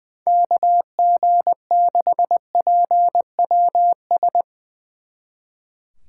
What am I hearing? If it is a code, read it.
Morse code, heard as a keyed tone, about 21 wpm, 700 Hz. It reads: KG6PWS
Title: Speed Racing - USA 2x3 Call Signs 20wpm